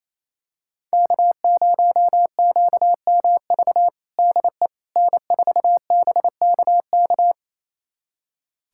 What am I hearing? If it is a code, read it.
Morse code, heard as a keyed tone, about 28 wpm, 700 Hz. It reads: K0QMV DE N4BKK
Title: USA Callsign DE Callsign - 5 Characters - 28wpm